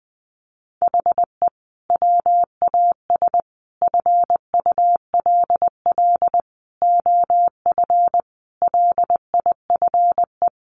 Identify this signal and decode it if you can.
Morse code, heard as a keyed tone, about 19 wpm, 700 Hz. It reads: HE WAS FULL OF LIFE
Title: Sentences from Top 1100 Words 20wpm